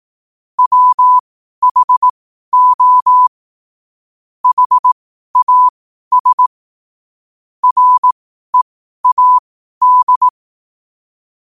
Morse code, heard as a keyed tone, about 18 wpm, spaced as for 12 wpm, 1000 Hz. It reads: WHO HAS READ
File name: qotdfiles / 12wpm